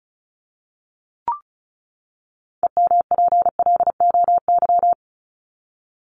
Morse code, heard as a keyed tone, about 34 wpm, 700 Hz. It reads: EMPLOY
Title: Top 1751-2000 Words 35wpm